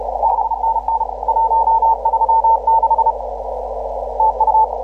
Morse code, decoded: 5NN 136 TU

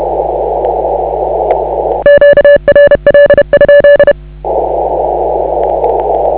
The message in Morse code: QRL?

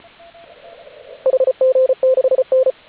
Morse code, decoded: HG6N

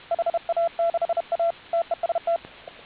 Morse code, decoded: HA6A TEST

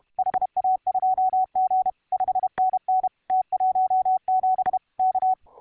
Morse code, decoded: HA2G 5NN T17 K

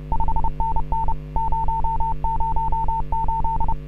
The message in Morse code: 5NN 008

